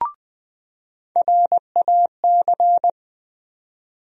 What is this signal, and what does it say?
Morse code, heard as a keyed tone, about 20 wpm, 700 Hz. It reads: RAC
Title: USA Call Sign Postfixes - 3 Letters 20wpm